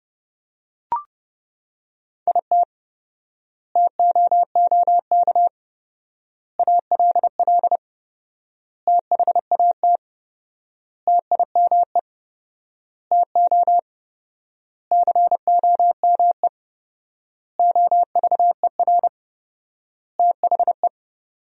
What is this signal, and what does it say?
Morse code, heard as a keyed tone, about 32 wpm, 700 Hz. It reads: IT TOOK ALL THAT TIME TO COME OVER THE
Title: 4x Word-Spacing - Sentences from Top 300 Words - 30wpm